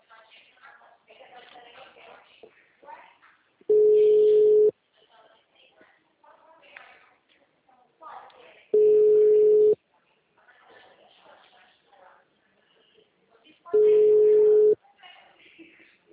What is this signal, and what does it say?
Ringback tone (Europe, 425 Hz), with speech.